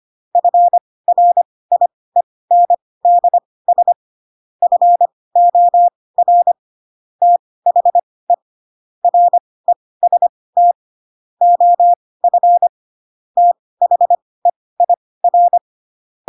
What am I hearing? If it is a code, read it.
Morse code, heard as a keyed tone, about 25 wpm, spaced as for 18 wpm, 700 Hz. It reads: FRIENDS FOR THE REST OF THEIR
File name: Pinocchio / Pinocchio ch2 12 25 ik1hge